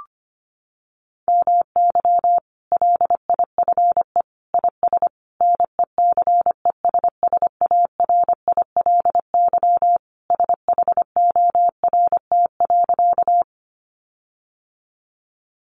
Morse code, heard as a keyed tone, about 25 wpm, 700 Hz. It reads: MY LIFE IS NECESSARILY SHORT.
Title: Sentences from Top 1400 Words 25wpm